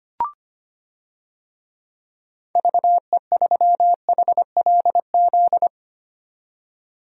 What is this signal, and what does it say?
Morse code, heard as a keyed tone, about 25 wpm, 700 Hz. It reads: VE3HLZ
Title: DXCC-100 Call Signs - 6 Characters - 25wpm